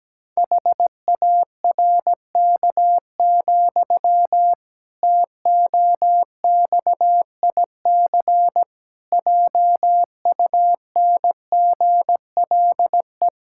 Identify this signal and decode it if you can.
Morse code, heard as a keyed tone, about 17 wpm, 700 Hz. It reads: HARK, TOXIC JUNGLE
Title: Pangram Challenge v2 - 17wpm